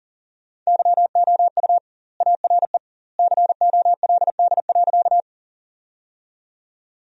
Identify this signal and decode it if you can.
Morse code, heard as a keyed tone, about 40 wpm, 700 Hz. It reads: YOU ARE COLD.
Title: 2x Word-Spacing - Sentences from Top 400 Words - 40wpm